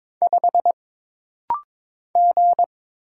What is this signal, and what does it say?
Morse code, heard as a keyed tone, about 22 wpm, 700 Hz. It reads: 5 G
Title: Single Letter-Number - Rapid-Fire 22wpm